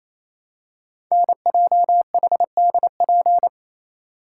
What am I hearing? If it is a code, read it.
Morse code, heard as a keyed tone, about 27 wpm, 700 Hz. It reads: NJHDP
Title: Code Groups - 5 Letters - 28wpm